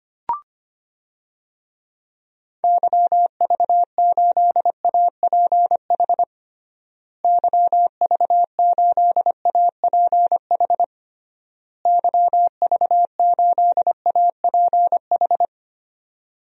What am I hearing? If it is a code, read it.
Morse code, heard as a keyed tone, about 25 wpm, 700 Hz. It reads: YV8APH YV8APH YV8APH
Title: DXCC-100 Call Signs - 6 Characters - Repeated 3x - 25wpm